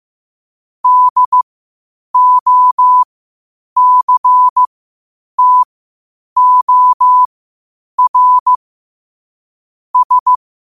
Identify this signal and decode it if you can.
Morse code, heard as a keyed tone, about 15 wpm, spaced as for 8 wpm, 1000 Hz. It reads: DOCTOR S
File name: qotdfiles / 8wpm